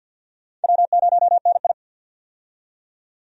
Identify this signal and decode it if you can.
Morse code, heard as a keyed tone, about 49 wpm, 700 Hz. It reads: W0NI
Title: USA Call Signs - 4 Characters - 50wpm